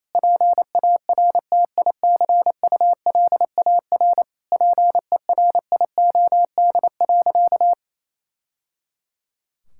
Morse code, heard as a keyed tone, about 28 wpm, 700 Hz. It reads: PARTICULAR PERIOD.
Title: Sentences from Top 1000 Words - Review Part 5 - 28wpm